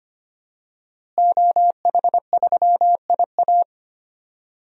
Morse code, heard as a keyed tone, about 25 wpm, 700 Hz. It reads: OH3IA
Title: DXCC-100 Call Signs - 5 Characters - Repeated 3x - 25wpm